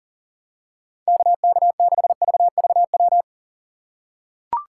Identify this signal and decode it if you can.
Morse code, heard as a keyed tone, about 37 wpm, 700 Hz. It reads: KK6VVW
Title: USA Call Signs - 6 Characters - 3x Character Spacing - 40wpm